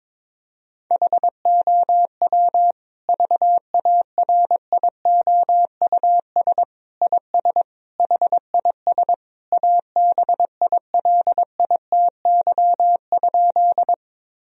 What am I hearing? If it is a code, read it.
Morse code, heard as a keyed tone, about 22 wpm, 700 Hz. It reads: HOW VARIOUS IS HIS ABILITY?
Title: Sentences from Top 1000 Words - Review Part 4 - 22wpm